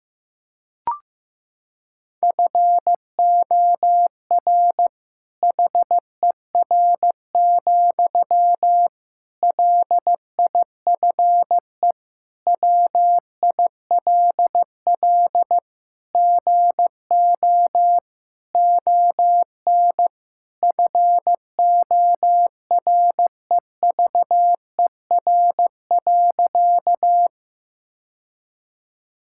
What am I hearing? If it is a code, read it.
Morse code, heard as a keyed tone, about 15 wpm, 700 Hz. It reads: FOR HER, LIFE WILL GO ON FOREVER.
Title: Sentences from Top 1600 Words 15wpm